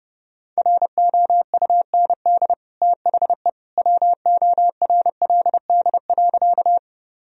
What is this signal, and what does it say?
Morse code, heard as a keyed tone, about 30 wpm, 700 Hz. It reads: ROUND THE WORLD.
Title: Sentences from Top 1000 Words - Review Part 1 - 30wpm